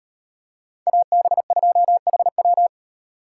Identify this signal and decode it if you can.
Morse code, heard as a keyed tone, about 37 wpm, 700 Hz. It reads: AB2HW